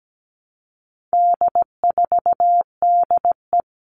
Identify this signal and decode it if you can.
Morse code, heard as a keyed tone, about 17 wpm, 700 Hz. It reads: D4DE
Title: All Cut Numbers - 4-Digit 17wpm